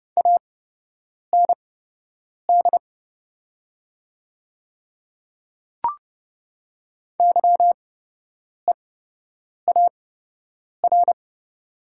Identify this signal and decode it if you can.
Morse code, heard as a keyed tone, about 30 wpm, spaced as for 8 wpm, 700 Hz. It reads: AND YEAR
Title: 8x Character Spacing - Top 300 Words - 30wpm